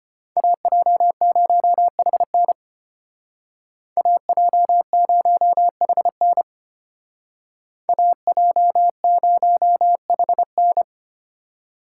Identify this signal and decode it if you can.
Morse code, heard as a keyed tone, about 30 wpm, 700 Hz. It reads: AJ0HN AJ0HN AJ0HN
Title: Speed Racing - USA 2x2 Call Signs 25wpm